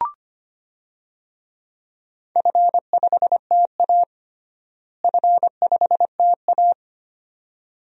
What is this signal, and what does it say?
Morse code, heard as a keyed tone, about 25 wpm, 700 Hz. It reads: F5TA F5TA
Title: DXCC-100 Call Signs - 4 Characters - Repeated 2x - 25wpm